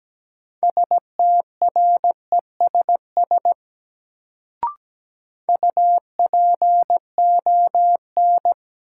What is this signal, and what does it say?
Morse code, heard as a keyed tone, about 17 wpm, 700 Hz. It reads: STRESS UPON
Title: Sentences from Top 1000 Words - Review Part 6 - New Word Drill - 17wpm